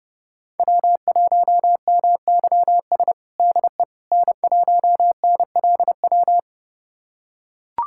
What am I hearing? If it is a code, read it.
Morse code, heard as a keyed tone, about 30 wpm, 700 Hz. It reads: W1MYS DE N1NLW
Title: USA Callsign DE Callsign - 5 Characters - 30wpm